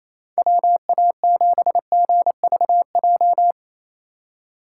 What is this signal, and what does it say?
Morse code, heard as a keyed tone, about 28 wpm, 700 Hz. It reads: WA7GVJ
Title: USA Call Signs - 6x Character Spacing - 28wpm